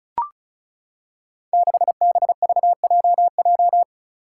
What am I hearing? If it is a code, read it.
Morse code, heard as a keyed tone, about 36 wpm, 700 Hz. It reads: 6BVJJ
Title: Five Challenging Letter-Number Combinations 35wpm